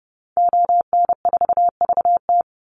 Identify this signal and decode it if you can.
Morse code, heard as a keyed tone, about 30 wpm, 700 Hz. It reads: ON4VT
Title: Call Signs - 30wpm